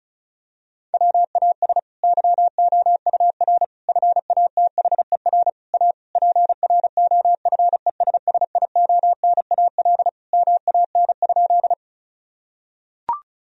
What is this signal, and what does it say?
Morse code, heard as a keyed tone, about 34 wpm, 700 Hz. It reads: WAS YOUR FATHER A PROFESSIONAL MAN?
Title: Sentences from Top 1000 Words - Review Part 4 - 35wpm